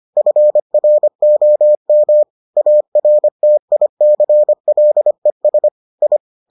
Morse code, heard as a keyed tone, about 25 wpm, 600 Hz. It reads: FROM ARTICLES I